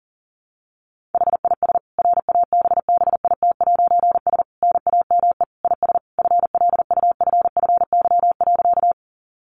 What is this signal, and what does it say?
Morse code, heard as a keyed tone, about 40 wpm, 700 Hz. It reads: HIS RABBIT'S NAME IS FLUFFY.
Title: Easy Phrases 40wpm